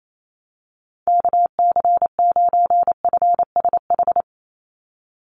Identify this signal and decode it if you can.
Morse code, heard as a keyed tone, about 28 wpm, 700 Hz. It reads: KC9FSH
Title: USA Call Signs 28wpm